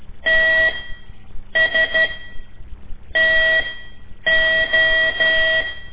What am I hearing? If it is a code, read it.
Morse code, heard as a keyed tone, about 12 wpm, spaced as for 7 wpm, 1950 Hz, with noise. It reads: TSTO
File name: ik1weg2 / CORSI CW / lez5-1